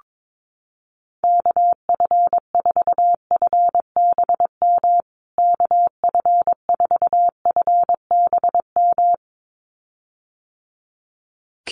Morse code, heard as a keyed tone, about 22 wpm, 700 Hz. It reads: KF4FBM KF4FBM
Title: USA Call Signs Repeated 2x 22wpm